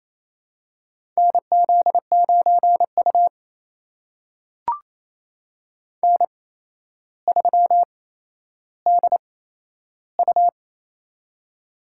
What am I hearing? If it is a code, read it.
Morse code, heard as a keyed tone, about 28 wpm, spaced as for 12 wpm, 700 Hz. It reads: NZ9U N3DU
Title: USA Call Signs - 4 Characters - 8x Character Spacing - 28wpm